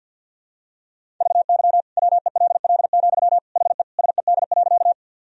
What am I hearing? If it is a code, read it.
Morse code, heard as a keyed tone, about 46 wpm, 700 Hz. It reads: VY WELL, HE SED.
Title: Sentences from Top 200 Words and Abbreviations 50wpm